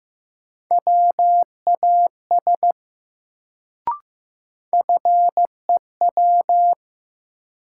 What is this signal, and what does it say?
Morse code, heard as a keyed tone, about 15 wpm, 700 Hz. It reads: WAS FEW
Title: Sets of 2 Words - New Word Drill - 15wpm